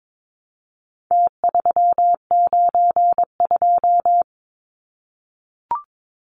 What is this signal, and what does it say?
Morse code, heard as a keyed tone, about 22 wpm, 700 Hz. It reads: T392
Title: Numbers - Contest Style - 4-Digit 22wpm